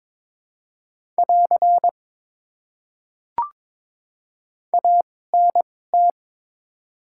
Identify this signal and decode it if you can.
Morse code, heard as a keyed tone, about 22 wpm, spaced as for 16 wpm, 700 Hz. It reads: <AR> ANT
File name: Top 90 QSO Elements - 2x Character Spacing - 22wpm